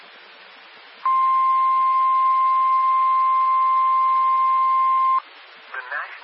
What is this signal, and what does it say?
Emergency Alert System (SAME) tones (the 1050 Hz attention tone), with speech. It is heard from 1.0 to 5.2 s.